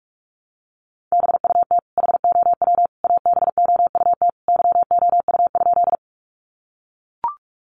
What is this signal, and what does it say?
Morse code, heard as a keyed tone, about 44 wpm, 700 Hz. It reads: BUT HOW ABOUT YOU?
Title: Sentences from Top 100 Words 45wpm